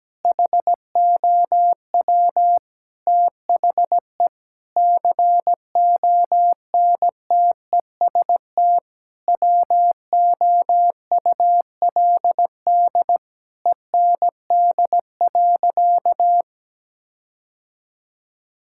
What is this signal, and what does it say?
Morse code, heard as a keyed tone, about 17 wpm, 700 Hz. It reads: HOW THE CONTEST WOULD END.
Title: Sentences from Top 1600 Words 17wpm